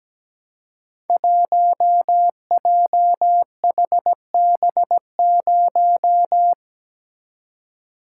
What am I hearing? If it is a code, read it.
Morse code, heard as a keyed tone, about 17 wpm, 700 Hz. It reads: 1JHB0
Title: Code Groups - 5 Letter:Numbers - 17wpm